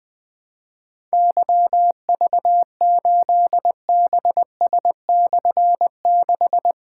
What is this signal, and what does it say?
Morse code, heard as a keyed tone, about 20 wpm, 700 Hz. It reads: YV8BS/6
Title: Difficult Call Signs 20wpm